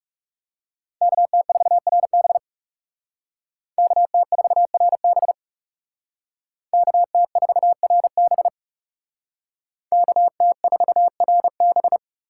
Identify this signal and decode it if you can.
Morse code, heard as a keyed tone, about 35 wpm, 700 Hz. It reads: KT4RB KT4RB KT4RB KT4RB
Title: Speed Racing - USA Call Signs 30wpm